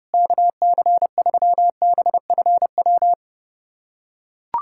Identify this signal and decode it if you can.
Morse code, heard as a keyed tone, about 32 wpm, 700 Hz. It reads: KC3BFW
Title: USA Callsign DE Callsign 30wpm